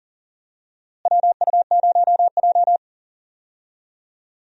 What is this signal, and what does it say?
Morse code, heard as a keyed tone, about 40 wpm, 700 Hz. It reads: WU0J
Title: USA Call Signs - 4 Characters - Repeated 3x - 40wpm